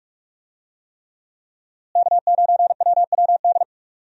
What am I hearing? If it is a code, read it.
Morse code, heard as a keyed tone, about 46 wpm, 700 Hz. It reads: K9WWD
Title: USA Call Signs - 5 Characters - Repeated 3x - 45wpm